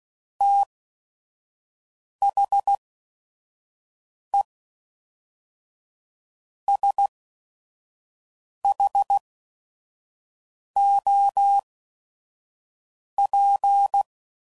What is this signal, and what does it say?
Morse code, heard as a keyed tone, about 16 wpm, spaced as for 5 wpm, 800 Hz. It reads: THE SHOP
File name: CW Code Groups @5WPM / Weather Sets / Weather 4 28